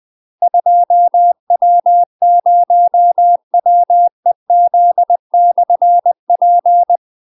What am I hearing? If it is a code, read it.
Morse code, heard as a keyed tone, about 20 wpm, 700 Hz. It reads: 2W0WEZ/P
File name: UK MASTER QSOs TRAINING / 20WPM / UK MASTER QSOs 0042